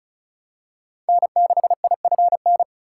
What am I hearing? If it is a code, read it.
Morse code, heard as a keyed tone, about 36 wpm, 700 Hz. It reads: N6IFN
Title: USA Call Signs - 3x Character Spacing - 35wpm